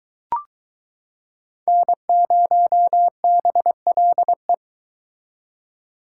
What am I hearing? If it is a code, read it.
Morse code, heard as a keyed tone, about 22 wpm, 700 Hz. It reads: N0BLE